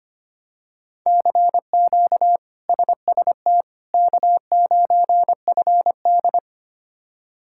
Morse code, heard as a keyed tone, about 25 wpm, 700 Hz. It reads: CQ SST K9FD
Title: K1USN SST QSO Practice - 25wpm